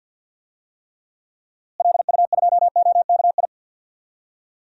Morse code, heard as a keyed tone, about 51 wpm, 700 Hz. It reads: LU2OKI
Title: DXCC-100 Call Signs - 6 Characters - Repeated 3x - 50wpm